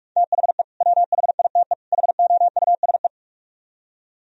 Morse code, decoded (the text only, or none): THE WHITE HOUSE